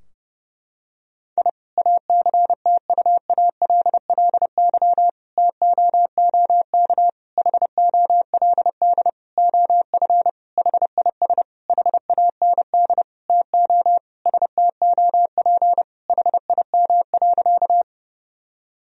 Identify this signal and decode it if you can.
Morse code, heard as a keyed tone, about 30 wpm, 700 Hz. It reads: I ACTUALLY TOOK HOLD OF HIS HAND TO STOP HIM.
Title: Sentences from Top 1000 Words - Review Part 4 - 30wpm